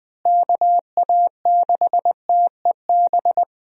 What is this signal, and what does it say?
Morse code, heard as a keyed tone, about 20 wpm, 700 Hz. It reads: KA6TEB
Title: USA Call Signs - 6 Characters - Repeated 3x - 20wpm